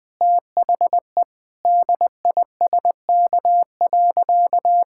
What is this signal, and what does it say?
Morse code, heard as a keyed tone, about 20 wpm, 700 Hz. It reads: THE DISK.
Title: Sentences from Top 1000 Words - Review Part 6 - 20wpm